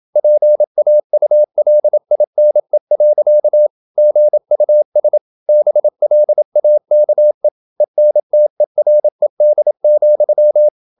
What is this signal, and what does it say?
Morse code, heard as a keyed tone, about 27 wpm, 600 Hz. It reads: PAULINE. GUS BLAKE ENTERED,
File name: Never 0027wpm